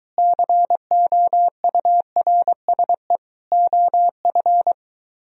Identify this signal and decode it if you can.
Morse code, heard as a keyed tone, about 23 wpm, 700 Hz. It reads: COURSE OF